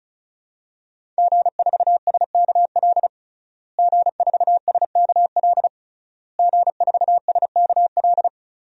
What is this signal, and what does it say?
Morse code, heard as a keyed tone, about 36 wpm, 700 Hz. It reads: G4SKL G4SKL G4SKL
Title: DXCC-100 Call Signs - 5 Characters - Repeated 3x - 35wpm